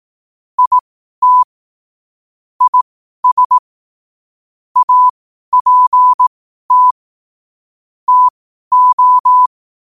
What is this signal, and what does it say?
Morse code, heard as a keyed tone, about 18 wpm, spaced as for 12 wpm, 1000 Hz. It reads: IT IS APT TO
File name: qotdfiles / 12wpm